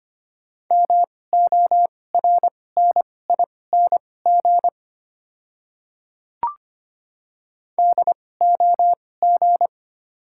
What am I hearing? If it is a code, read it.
Morse code, heard as a keyed tone, about 25 wpm, spaced as for 18 wpm, 700 Hz. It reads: MORNING DOG
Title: 2x Character Spacing - Top 500 Words - 25wpm